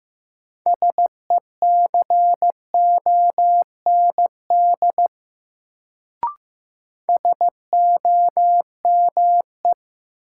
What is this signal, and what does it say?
Morse code, heard as a keyed tone, about 15 wpm, 700 Hz. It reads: SECOND SOME
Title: Sets of 2 Words - New Word Drill - 15wpm